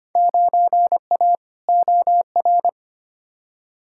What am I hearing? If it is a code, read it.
Morse code, heard as a keyed tone, about 25 wpm, 700 Hz. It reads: 9A OR
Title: Field Day Exchanges Encore - 25wpm